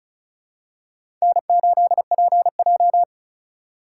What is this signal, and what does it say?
Morse code, heard as a keyed tone, about 36 wpm, 700 Hz. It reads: N8PJ
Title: USA Call Signs - 4 Characters - 2x Character Spacing - 35wpm